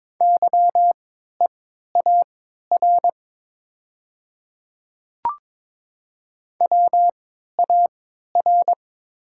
Morse code, heard as a keyed tone, about 22 wpm, spaced as for 12 wpm, 700 Hz. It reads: YEAR WAR
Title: 3x Character Spacing - Top 500 Words - 22wpm